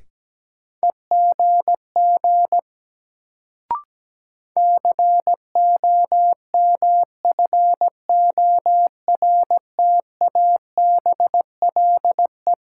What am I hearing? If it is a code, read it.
Morse code, heard as a keyed tone, about 17 wpm, 700 Hz. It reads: EGG COMFORTABLE
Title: Sentences from Top 1000 Words - Review Part 7 - New Word Drill - 17wpm